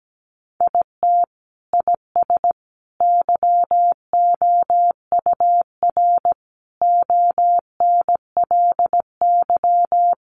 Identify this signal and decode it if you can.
Morse code, heard as a keyed tone, about 17 wpm, 700 Hz. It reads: IT IS YOUR ONLY
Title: Sentences from Top 800 Words 17wpm